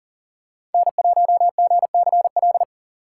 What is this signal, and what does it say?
Morse code, heard as a keyed tone, about 40 wpm, 700 Hz. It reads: N1GCL
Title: USA Call Signs - 5 Characters - 3x Character Spacing - 40wpm